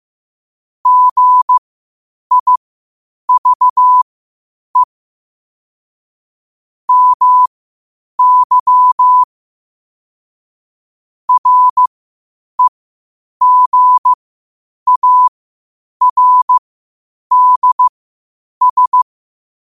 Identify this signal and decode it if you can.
Morse code, heard as a keyed tone, about 15 wpm, spaced as for 8 wpm, 1000 Hz. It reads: GIVE MY REGARDS
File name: qotdfiles / 8wpm